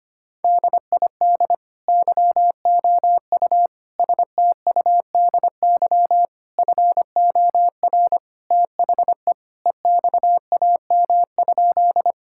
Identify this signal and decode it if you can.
Morse code, heard as a keyed tone, about 25 wpm, 700 Hz. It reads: DID YOU STUDY FOR THE EXAM?
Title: Sentences from Top 1000 Words - Review Part 5 - 25wpm